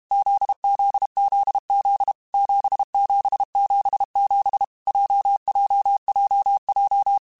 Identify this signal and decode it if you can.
Morse code, heard as a keyed tone, about 32 wpm, 800 Hz. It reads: ZZZZ 7777 JJJJ
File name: Basic Letter Groups By7Groups / JZ17 / JZ17@30